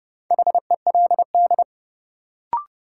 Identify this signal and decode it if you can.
Morse code, heard as a keyed tone, about 30 wpm, 700 Hz. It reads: HELD
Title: Sentences from Top 1600 Words - New Word Drill Alt - 30wpm